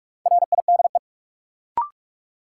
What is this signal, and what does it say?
Morse code, heard as a keyed tone, about 46 wpm, 700 Hz. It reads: RIDE
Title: Sentences from Top 1200 Words - New Word Drill Alt - 45wpm